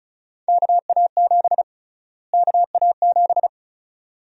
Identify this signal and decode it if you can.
Morse code, heard as a keyed tone, about 36 wpm, 700 Hz. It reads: KA7 KA7
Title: USA Call Sign Prefixes - Repeated 2x - 35wpm